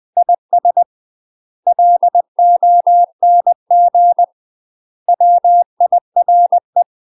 Morse code, heard as a keyed tone, about 20 wpm, 700 Hz. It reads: IS LONG WIRE
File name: UK MASTER QSOs TRAINING / 20WPM / UK MASTER QSOs 0008